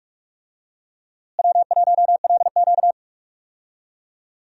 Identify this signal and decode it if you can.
Morse code, heard as a keyed tone, about 42 wpm, 700 Hz. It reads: W1LQ
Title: USA Call Signs - 4 Characters - Repeated 3x - 45wpm